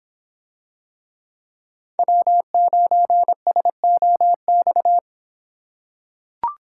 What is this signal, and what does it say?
Morse code, heard as a keyed tone, about 24 wpm, 700 Hz. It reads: W9SOX